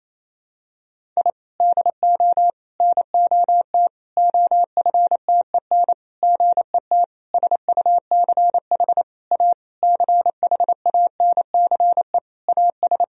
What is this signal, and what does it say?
Morse code, heard as a keyed tone, about 28 wpm, 700 Hz. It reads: I DO NOT OFTEN GET SUCH A CHANCE AS
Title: Sentences from Top 1000 Words - Review Part 1 - 28wpm